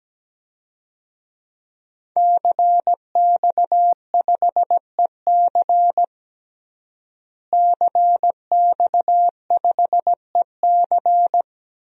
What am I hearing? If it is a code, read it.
Morse code, heard as a keyed tone, about 17 wpm, 700 Hz. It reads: CX5EC CX5EC
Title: DXCC-100 Call Signs - 5 Characters - Repeated 3x - 17wpm